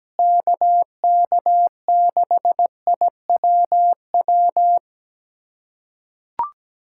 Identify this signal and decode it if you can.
Morse code, heard as a keyed tone, about 17 wpm, 700 Hz. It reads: KK6IWW
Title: USA Callsign DE Callsign - 6 Characters - 17wpm